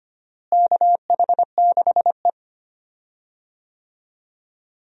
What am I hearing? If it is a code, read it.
Morse code, heard as a keyed tone, about 25 wpm, 700 Hz. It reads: KH6E
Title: DXCC-100 Call Signs - 4 Characters - Repeated 2x - 25wpm